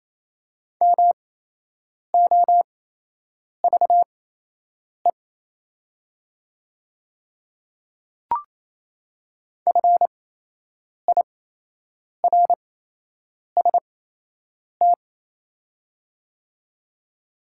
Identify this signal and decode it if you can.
Morse code, heard as a keyed tone, about 27 wpm, spaced as for 8 wpm, 700 Hz. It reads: MOVE FIRST
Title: 8x Character Spacing - Top 200 Words - 28wpm